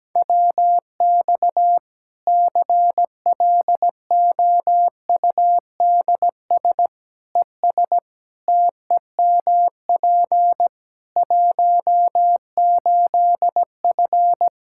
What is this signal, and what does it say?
Morse code, heard as a keyed tone, about 17 wpm, 700 Hz. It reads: WX CLOUDS ES TEMP 18F
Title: Practice QSO - Exchange 2 - Info - 17wpm